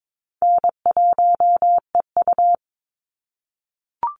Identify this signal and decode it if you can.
Morse code, heard as a keyed tone, about 22 wpm, 700 Hz. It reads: N1EU
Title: NA SOTA Activator Callsigns 22wpm